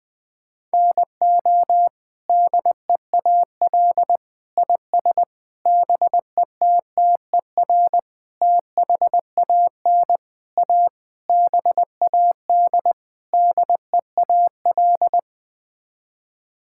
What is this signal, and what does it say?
Morse code, heard as a keyed tone, about 20 wpm, 700 Hz. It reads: NO DEAL IS BETTER THAN A BAD DEAL
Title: Sets of 8 Words - 20wpm